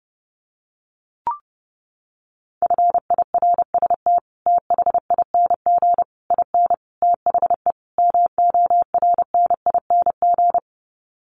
Morse code, heard as a keyed tone, about 30 wpm, 700 Hz. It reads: FIRST THING IN THE MORNING
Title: Sets of 5 Words Encore2 30wpm